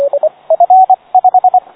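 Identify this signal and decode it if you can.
Morse code, heard as a keyed tone, about 24 wpm, 730 Hz, with radio noise. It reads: N 5
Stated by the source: Morse received on the amateur bands with a narrow CW filter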